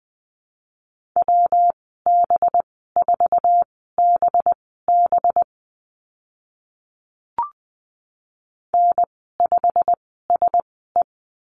Morse code, heard as a keyed tone, about 20 wpm, spaced as for 14 wpm, 700 Hz. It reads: WB4BB N5SE
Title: 207 Lesson 33g - Letter j - Call Signs 20wpm